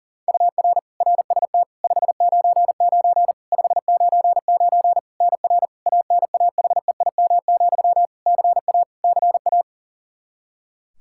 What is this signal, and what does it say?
Morse code, heard as a keyed tone, about 40 wpm, 700 Hz. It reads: UR RST 599 599 NR ANAHEIM, CA CA
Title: Practice QSO - Exchange 1 - Beginning - 40wpm